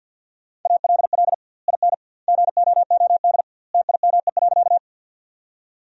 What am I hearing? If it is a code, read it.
Morse code, heard as a keyed tone, about 53 wpm, 700 Hz. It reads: ALL IN GOOD TIME.
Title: 2x Word-Spacing - Sentences from Top 200 Words - 50wpm